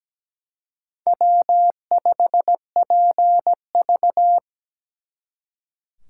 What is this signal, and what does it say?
Morse code, heard as a keyed tone, about 20 wpm, 700 Hz. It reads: W5PV
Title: Speed Racing - USA 1x2 Call Signs 17wpm